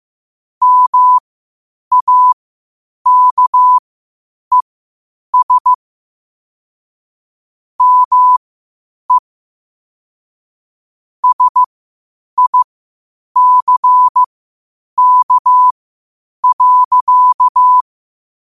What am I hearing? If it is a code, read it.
Morse code, heard as a keyed tone, about 15 wpm, spaced as for 8 wpm, 1000 Hz. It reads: MAKES ME SICK.
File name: qotdfiles / 8wpm